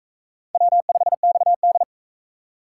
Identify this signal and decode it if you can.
Morse code, heard as a keyed tone, about 42 wpm, 700 Hz. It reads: W5XD